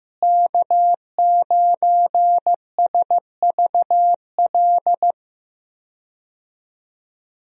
Morse code, heard as a keyed tone, about 15 wpm, 700 Hz. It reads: K9SVL
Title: USA Call Signs - 5 Characters - Repeated 3x - 15wpm